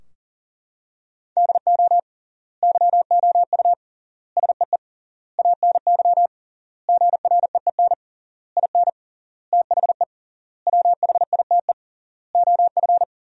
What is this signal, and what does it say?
Morse code, heard as a keyed tone, about 40 wpm, 700 Hz. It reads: DO YOU SEE ANY GREEN IN THE WHITE OF
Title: 3x Word-Spacing - Sentences from Top 500 Words - 40wpm